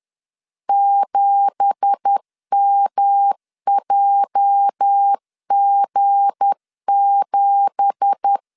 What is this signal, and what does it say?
Morse code, heard as a keyed tone, about 10 wpm, 800 Hz. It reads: 7MJG7
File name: LEZIONI / 9 2867 10wpm 2733